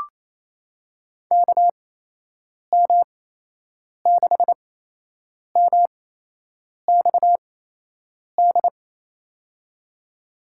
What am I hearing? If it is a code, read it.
Morse code, heard as a keyed tone, about 27 wpm, spaced as for 8 wpm, 700 Hz. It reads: KM6MXD
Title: USA Call Signs - 6 Characters - 8x Character Spacing - 28wpm